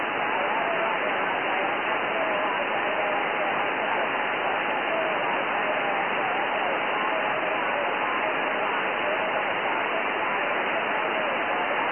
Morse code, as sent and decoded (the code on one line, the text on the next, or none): none